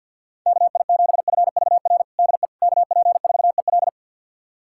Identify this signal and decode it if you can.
Morse code, heard as a keyed tone, about 52 wpm, 700 Hz. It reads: KI7FVR DE KP4EL
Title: USA Callsign DE Callsign 50wpm